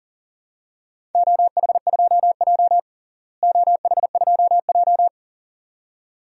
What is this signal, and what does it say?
Morse code, heard as a keyed tone, about 40 wpm, 700 Hz. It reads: OH2J OH2J
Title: DXCC-100 Call Signs - 4 Characters - Repeated 2x - 40wpm